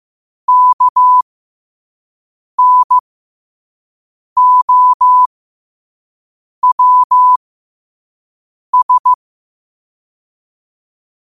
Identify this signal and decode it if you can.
Morse code, heard as a keyed tone, about 15 wpm, spaced as for 5 wpm, 1000 Hz. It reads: KNOWS
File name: qotdfiles / 5wpm